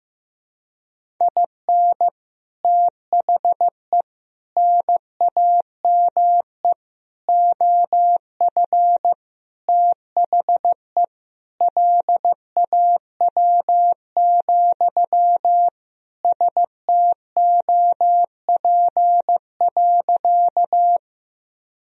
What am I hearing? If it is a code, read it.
Morse code, heard as a keyed tone, about 15 wpm, 700 Hz. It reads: IN THE NAME OF THE LAW, STOP.
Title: Sentences from Top 1000 Words - Review Part 3 - 15wpm